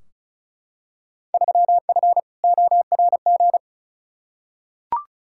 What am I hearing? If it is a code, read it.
Morse code, heard as a keyed tone, about 36 wpm, 700 Hz. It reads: 3F ORG
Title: Field Day Exchanges - 35wpm